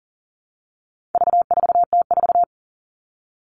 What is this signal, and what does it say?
Morse code, heard as a keyed tone, about 40 wpm, 700 Hz. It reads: V4T4
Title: All Cut Numbers - 4-Digit 40wpm